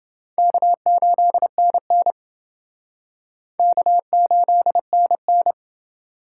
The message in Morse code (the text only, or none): K8NN K8NN